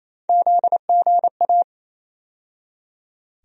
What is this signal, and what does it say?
Morse code, heard as a keyed tone, about 28 wpm, 700 Hz. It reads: ZGA
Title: USA Call Sign Postfixes - 3 Letters 28wpm